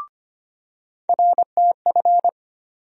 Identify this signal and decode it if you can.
Morse code, heard as a keyed tone, about 25 wpm, 700 Hz. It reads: RTF
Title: USA Call Sign Postfixes - 3 Letters 25wpm